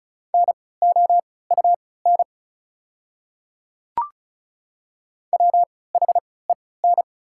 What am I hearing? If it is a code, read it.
Morse code, heard as a keyed tone, about 36 wpm, spaced as for 20 wpm, 700 Hz. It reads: NOUN WHEN
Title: 3x Character Spacing - Top 500 Words - 35wpm